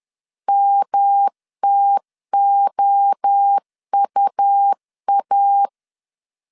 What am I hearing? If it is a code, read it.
Morse code, heard as a keyed tone, about 10 wpm, 800 Hz. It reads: MTOUA
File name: CW / DET / 1 3 1500C S C 10wpm 3335